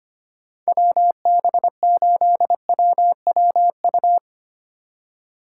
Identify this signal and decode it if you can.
Morse code, heard as a keyed tone, about 32 wpm, 700 Hz. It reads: WB8WWU